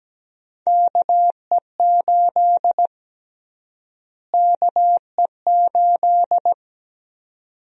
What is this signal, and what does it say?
Morse code, heard as a keyed tone, about 17 wpm, 700 Hz. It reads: KE8 KE8
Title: USA Call Sign Prefixes - Repeated 2x - 17wpm